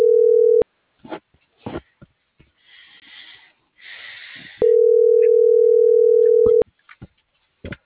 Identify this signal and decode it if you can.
Ringback tone (North America), 440 + 480 Hz.